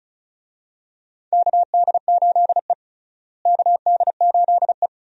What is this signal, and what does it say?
Morse code, heard as a keyed tone, about 34 wpm, 700 Hz. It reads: KD8E KD8E
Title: USA Call Signs - 4 Characters - Repeated 2x - 35wpm